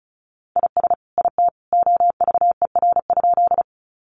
Morse code, heard as a keyed tone, about 34 wpm, 700 Hz. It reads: IS IT OVER?
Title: Sentences from Top 100 Words 35wpm